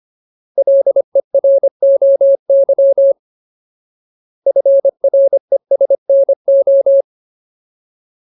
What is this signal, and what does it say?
Morse code, heard as a keyed tone, about 25 wpm, 550 Hz. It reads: LEROY FRESNO
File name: Ultimate Word List / 25WPM Ultimate